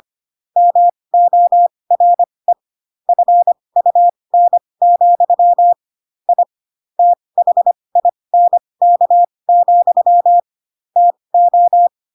Morse code, heard as a keyed tone, about 25 wpm, spaced as for 20 wpm, 700 Hz. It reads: MORE FUN, I THINK, TO
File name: Pinocchio / Pinocchio ch4 15 25 ik1hge